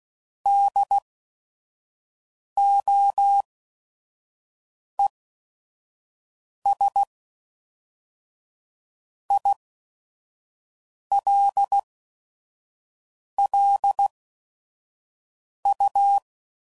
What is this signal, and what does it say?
Morse code, heard as a keyed tone, about 16 wpm, spaced as for 5 wpm, 800 Hz. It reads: DOES ILLU
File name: CW Code Groups @5WPM / Technical Subjects / AND Gate